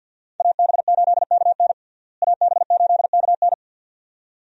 Morse code, heard as a keyed tone, about 47 wpm, 700 Hz. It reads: AB8KN AB8KN
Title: USA Call Signs - 5 Characters - Repeated 2x - 50wpm